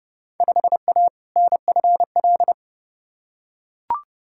Morse code, heard as a keyed tone, about 30 wpm, 700 Hz. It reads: 5A NFL
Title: Field Day Exchanges - 30wpm